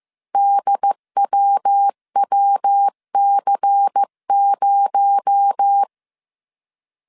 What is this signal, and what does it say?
Morse code, heard as a keyed tone, about 15 wpm, 800 Hz. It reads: DWWC0
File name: CW / DET / 1 6 1500C S C 14wpm 2951